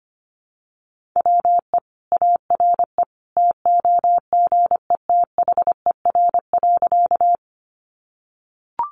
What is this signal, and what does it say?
Morse code, heard as a keyed tone, about 25 wpm, 700 Hz. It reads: WE ARE TOGETHER.
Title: Sentences from Top 300 Words 25wpm